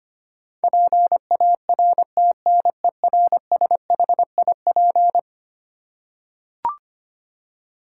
Morse code, heard as a keyed tone, about 25 wpm, 700 Hz. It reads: PARTNERSHIP
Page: morse-code-ninja-practice-25wpm